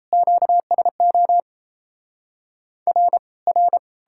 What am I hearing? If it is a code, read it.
Morse code, heard as a keyed tone, about 37 wpm, 700 Hz. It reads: QSO R R